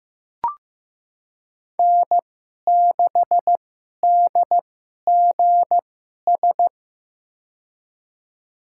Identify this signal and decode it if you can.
Morse code, heard as a keyed tone, about 15 wpm, spaced as for 11 wpm, 700 Hz. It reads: N6DGS
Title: USA Call Signs - 5 Characters - 2x Character Spacing - 15wpm